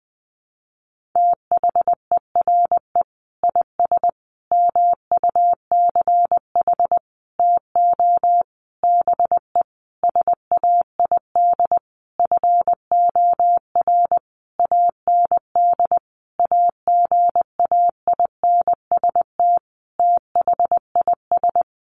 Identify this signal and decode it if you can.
Morse code, heard as a keyed tone, about 20 wpm, 700 Hz. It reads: THERE IS MUCH TO BE SAID FOR AND AGAINST THIS
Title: Sentences from Top 1300 Words 20wpm